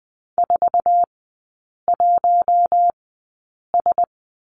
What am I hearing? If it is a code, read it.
Morse code, heard as a keyed tone, about 20 wpm, spaced as for 8 wpm, 700 Hz. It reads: 41S
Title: 047 Lesson 9e - Number 4 - Three Characters 20wpm